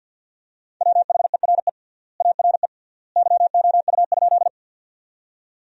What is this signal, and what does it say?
Morse code, heard as a keyed tone, about 52 wpm, 700 Hz. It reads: WHERE ARE YOU?
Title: 3x Word-Spacing - Sentences from Top 200 Words - 50wpm